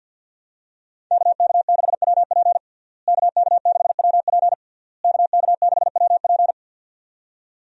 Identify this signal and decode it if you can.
Morse code, heard as a keyed tone, about 50 wpm, 700 Hz. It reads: KK6WP KK6WP KK6WP
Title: USA Call Signs - 5 Characters - Repeated 3x - 50wpm